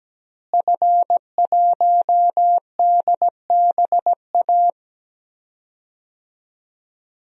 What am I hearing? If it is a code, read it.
Morse code, heard as a keyed tone, about 17 wpm, 700 Hz. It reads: F1DBA
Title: DXCC-100 Call Signs - 5 Characters - Repeated 3x - 17wpm